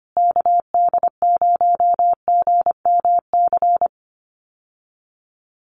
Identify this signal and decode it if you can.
Morse code, heard as a keyed tone, about 25 wpm, 700 Hz. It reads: KD0GMC
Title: USA Call Signs Repeated 3x 25wpm